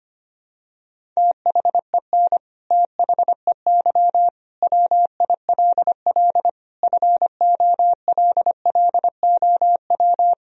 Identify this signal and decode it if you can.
Morse code, heard as a keyed tone, about 25 wpm, 700 Hz. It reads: THEN THEY WILL FOLLOW
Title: Sentences from Top 200 Words and Abbreviations 25wpm